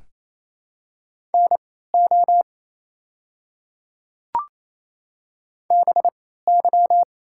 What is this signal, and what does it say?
Morse code, heard as a keyed tone, about 28 wpm, spaced as for 16 wpm, 700 Hz. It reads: NO BY
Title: 3x Character Spacing - Top 100 Words - 28wpm